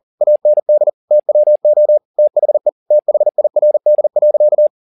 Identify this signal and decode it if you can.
Morse code, heard as a keyed tone, about 40 wpm, 600 Hz. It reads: AND TWO THE THIRD.
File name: the old man and the sea 2 40wpm